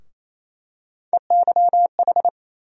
Morse code, heard as a keyed tone, about 28 wpm, 700 Hz. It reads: EYH
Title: USA Call Sign Postfixes - 3 Letters 28wpm